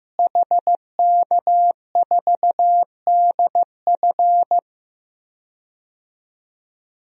Morse code, heard as a keyed tone, about 15 wpm, 700 Hz. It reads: HK4DF
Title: DXCC-100 Call Signs - 5 Characters - Repeated 2x - 15wpm